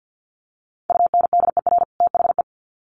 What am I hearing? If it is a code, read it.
Morse code, heard as a keyed tone, about 48 wpm, 700 Hz. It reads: UNDER THE
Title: Sets of 2 Words 50wpm